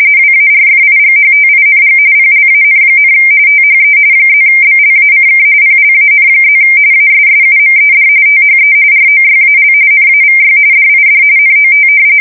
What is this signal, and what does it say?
RTTY, about 45.45 baud. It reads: WARFIGHTING PLAN, THE +BLACK PLAN+ ASSUMED THE ROYAL NAVY DID NOT
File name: fsk170 / usnavy4